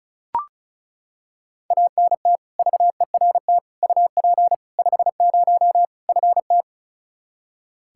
Morse code, heard as a keyed tone, about 36 wpm, 700 Hz. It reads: ANT VERT UP 50 FT
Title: Practice QSO - Exchange 2 - Info - 35wpm